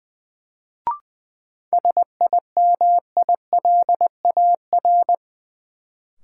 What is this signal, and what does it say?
Morse code, heard as a keyed tone, about 20 wpm, 700 Hz. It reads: SIMILAR
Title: Sentences from Top 1000 Words - Review Part 5 - New Word Drill - 20wpm